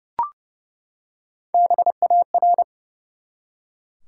Morse code, heard as a keyed tone, about 32 wpm, 700 Hz. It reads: BAR
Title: Three Letter Words - 30wpm